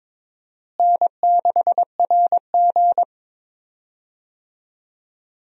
Morse code, heard as a keyed tone, about 22 wpm, 700 Hz. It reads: N6RG
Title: USA Call Signs - 4 Characters - Repeated 2x - 22wpm